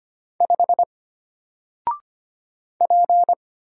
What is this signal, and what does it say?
Morse code, heard as a keyed tone, about 25 wpm, 700 Hz. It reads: 5 P
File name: Single-Letter-Number-25wpm